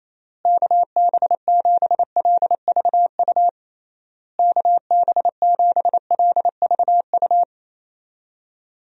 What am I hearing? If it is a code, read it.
Morse code, heard as a keyed tone, about 27 wpm, 700 Hz. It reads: KB7LVU KB7LVU
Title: USA Call Signs - 6 Characters - Repeated 3x - 28wpm